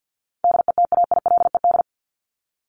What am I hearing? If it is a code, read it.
Morse code, heard as a keyed tone, about 50 wpm, 700 Hz. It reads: DETAILED
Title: Top 1251-1500 Words 50wpm